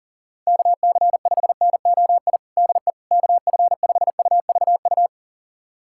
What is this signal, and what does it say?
Morse code, heard as a keyed tone, about 40 wpm, 700 Hz. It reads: KC5NOI DE KF5UVU
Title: USA Callsign DE Callsign - 6 Characters - 40wpm